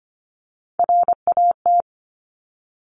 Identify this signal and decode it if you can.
Morse code, heard as a keyed tone, about 25 wpm, 700 Hz. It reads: RAT
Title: Top Three Letter Combinations in Words 25wpm